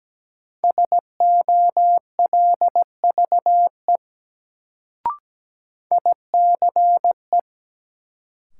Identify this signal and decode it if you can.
Morse code, heard as a keyed tone, about 17 wpm, 700 Hz. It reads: SOLVE ICE
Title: Sentences from Top 1000 Words - New Word Drill - 17wpm